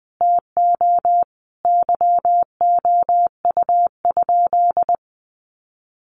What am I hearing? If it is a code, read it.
Morse code, heard as a keyed tone, about 20 wpm, 700 Hz. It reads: TO YOU?
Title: Sentences from Top 1300 Words 20wpm